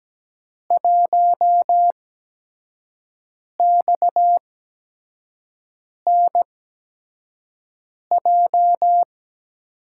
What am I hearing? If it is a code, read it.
Morse code, heard as a keyed tone, about 17 wpm, spaced as for 5 wpm, 700 Hz. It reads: 1XNJ
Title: USA Call Signs - 5 Characters - 8x Character Spacing - 17wpm